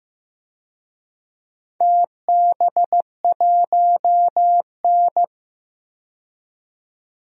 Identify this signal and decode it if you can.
Morse code, heard as a keyed tone, about 15 wpm, 700 Hz. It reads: TB1N
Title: DXCC-100 Call Signs - 4 Characters - 15wpm